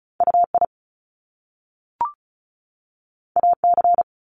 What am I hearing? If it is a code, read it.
Morse code, heard as a keyed tone, about 36 wpm, 700 Hz. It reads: UI AC
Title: Top Two Letter Combinations in Words 35wpm